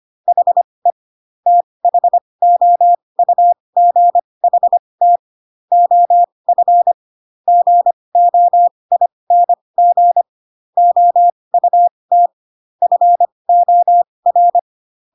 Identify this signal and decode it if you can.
Morse code, heard as a keyed tone, about 25 wpm, spaced as for 20 wpm, 700 Hz. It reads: HE THOUGHT OF GOING OUT FOR